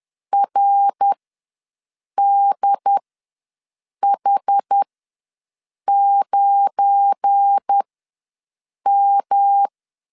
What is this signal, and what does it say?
Morse code, heard as a keyed tone, about 10 wpm, spaced as for 6 wpm, 800 Hz. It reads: RDH9M
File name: LEZIONI / 7 GF9Z / 7 GF9 ws 10wpm 4034